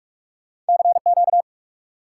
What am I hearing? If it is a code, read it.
Morse code, heard as a keyed tone, about 60 wpm, spaced as for 52 wpm, 700 Hz. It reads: CQ
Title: Speed-Racing Top 90 QSO Elements 45wpm